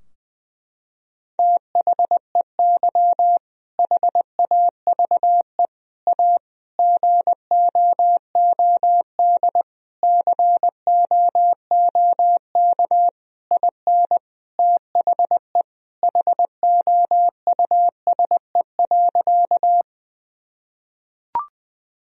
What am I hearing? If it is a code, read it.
Morse code, heard as a keyed tone, about 20 wpm, 700 Hz. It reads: THEY HAVE A GOOD COOK IN THE HOUSE.
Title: Sentences from Top 1000 Words - Review Part 4 - 20wpm